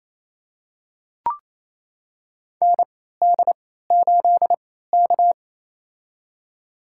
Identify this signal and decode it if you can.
Morse code, heard as a keyed tone, about 27 wpm, spaced as for 16 wpm, 700 Hz. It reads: ND8K
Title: USA Call Signs - 4 Characters - 3x Character Spacing - 28wpm